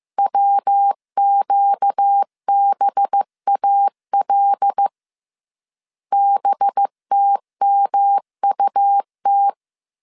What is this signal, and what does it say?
Morse code, heard as a keyed tone, about 15 wpm, 800 Hz. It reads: WQBAL BTMUT
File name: CW / DET / 1 6 1500C S B 14wpm 2958